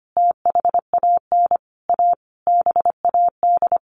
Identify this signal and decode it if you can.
Morse code, heard as a keyed tone, about 25 wpm, 700 Hz. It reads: THAN A BAD
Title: Sentences from Top 1500 Words 25wpm